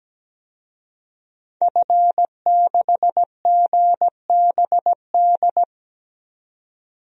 Morse code, heard as a keyed tone, about 17 wpm, 700 Hz. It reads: F6GBD
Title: DXCC-100 Call Signs - 5 Characters - 17wpm